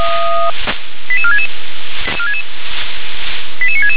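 Five-tone selective call (ZVEI), heard as quick runs of five tones, tones of about 70 ms. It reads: 88466 88966
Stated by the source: ZVEI five-tone call received by SDR.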